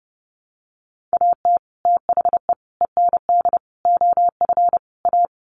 Morse code, heard as a keyed tone, about 30 wpm, 700 Hz. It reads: AT THE END OF A
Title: Sets of 5 Words 30wpm